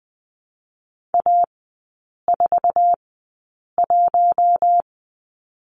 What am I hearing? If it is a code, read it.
Morse code, heard as a keyed tone, about 20 wpm, spaced as for 8 wpm, 700 Hz. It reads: A41
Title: 053 Lesson 10e - Letter r - Three Characters 20wpm